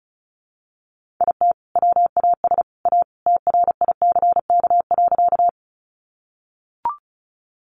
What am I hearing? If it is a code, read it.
Morse code, heard as a keyed tone, about 34 wpm, 700 Hz. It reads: IT WAS A TRICK.
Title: Sentences from Top 1500 Words 35wpm